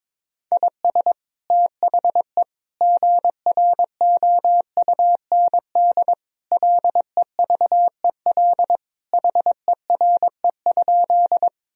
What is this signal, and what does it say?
Morse code, heard as a keyed tone, about 22 wpm, 700 Hz. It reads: IS THE GROUND LEVEL HERE?
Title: Sentences from Top 1000 Words - Review Part 4 - 22wpm